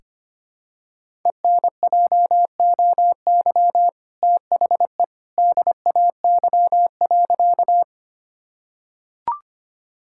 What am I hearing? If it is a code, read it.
Morse code, heard as a keyed tone, about 25 wpm, 700 Hz. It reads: ENJOY THE DAY.
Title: Sentences from Top 1000 Words - Review Part 5 - 25wpm